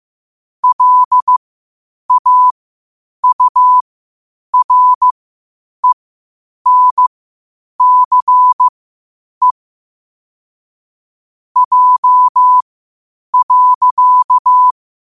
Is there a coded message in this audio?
Yes: Morse code from 0.6 to 14.7 s, heard as a keyed tone, about 15 wpm, spaced as for 8 wpm, 1000 Hz. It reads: LAURENCE J.